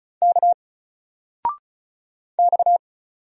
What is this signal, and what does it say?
Morse code, heard as a keyed tone, about 36 wpm, 700 Hz. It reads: K X
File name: Single-Letter-Number-35wpm